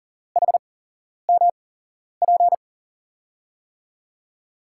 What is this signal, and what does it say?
Morse code, heard as a keyed tone, about 40 wpm, spaced as for 11 wpm, 700 Hz. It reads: HMP
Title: USA Call Signs - 6 Characters - 8x Character Spacing - 40wpm